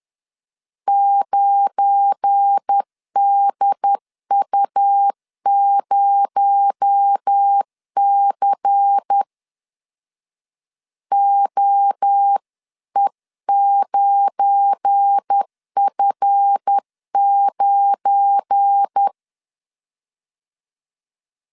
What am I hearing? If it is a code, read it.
Morse code, heard as a keyed tone, about 10 wpm, 800 Hz. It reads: 9DU0C OE9F9
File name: LEZIONI / 7 GF9Z / 7 GF9 10wpm 2747